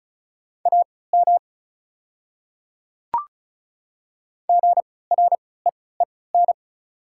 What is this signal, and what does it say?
Morse code, heard as a keyed tone, about 36 wpm, spaced as for 20 wpm, 700 Hz. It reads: AM GREEN
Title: 3x Character Spacing - Top 500 Words - 35wpm